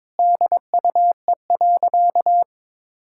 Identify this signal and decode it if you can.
Morse code, heard as a keyed tone, about 22 wpm, 700 Hz. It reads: DUE.
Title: Sentences from Top 1000 Words - Review Part 6 - 22wpm